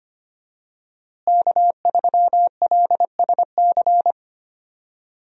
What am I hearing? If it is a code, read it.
Morse code, heard as a keyed tone, about 25 wpm, 700 Hz. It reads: K3LSC
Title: USA Call Signs - 5 Characters - Repeated 3x - 25wpm